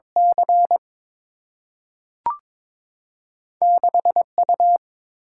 Morse code, heard as a keyed tone, about 22 wpm, 700 Hz. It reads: C 6 U
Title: Single-Letter Single-Number 22wpm